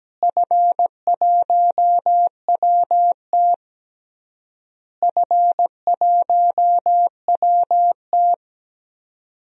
Morse code, heard as a keyed tone, about 17 wpm, 700 Hz. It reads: F1WT F1WT
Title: DXCC-100 Call Signs - 4 Characters - Repeated 3x - 17wpm